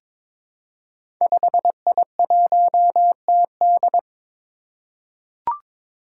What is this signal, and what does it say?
Morse code, heard as a keyed tone, about 22 wpm, 700 Hz. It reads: 5I1TD
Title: Five Challenging Letter-Number Combinations 22wpm